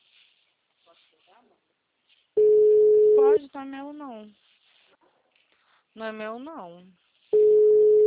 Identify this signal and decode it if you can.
Ringback tone (Europe, 425 Hz), with speech.